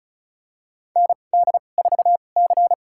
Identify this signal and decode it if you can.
Morse code, heard as a keyed tone, about 34 wpm, spaced as for 25 wpm, 700 Hz. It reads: ND4C
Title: USA Call Signs - 4 Characters - 2x Character Spacing - 35wpm